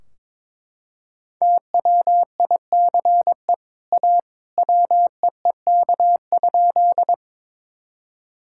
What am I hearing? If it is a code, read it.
Morse code, heard as a keyed tone, about 22 wpm, 700 Hz. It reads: TWICE A WEEK?
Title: Sentences from Top 1600 Words 22wpm